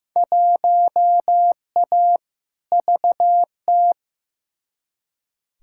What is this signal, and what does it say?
Morse code, heard as a keyed tone, about 15 wpm, 700 Hz. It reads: 1A VT
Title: Field Day Exchanges Encore - 15wpm